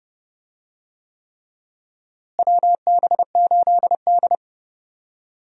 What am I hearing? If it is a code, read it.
Morse code, heard as a keyed tone, about 30 wpm, 700 Hz. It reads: WB8D
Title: USA Call Signs - 4 Characters - Repeated 3x - 30wpm